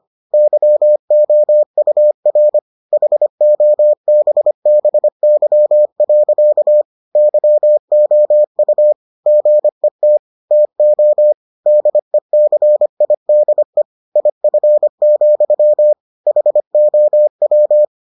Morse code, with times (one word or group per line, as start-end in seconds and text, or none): 0.3-2.6 YOUR
2.9-6.8 HOBBY.
7.2-8.9 YOU
9.3-10.2 GET
10.5-11.3 TO
11.7-13.8 DECIDE
14.2-15.9 IF,
16.3-17.9 HOW